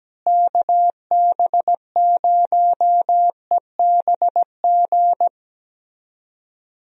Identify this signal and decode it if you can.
Morse code, heard as a keyed tone, about 17 wpm, 700 Hz. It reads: KB0EBG
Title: USA Call Signs - 6 Characters - Repeated 2x - 17wpm